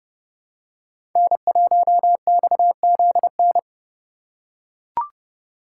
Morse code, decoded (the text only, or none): N1XZN